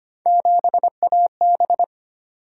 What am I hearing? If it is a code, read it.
Morse code, heard as a keyed tone, about 30 wpm, 700 Hz. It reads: 7AB
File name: Speed-Racing - Three Letter-Number Combinations - 25wpm